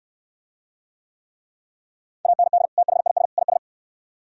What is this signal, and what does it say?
Morse code, heard as a keyed tone, about 68 wpm, 700 Hz. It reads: AND THIS IS